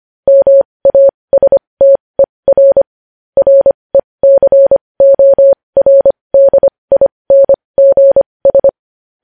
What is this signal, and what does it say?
Morse code, heard as a keyed tone, about 25 wpm, spaced as for 20 wpm, 550 Hz. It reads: MASTER RECORDINGS
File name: podcast / News headlines-1x-25x20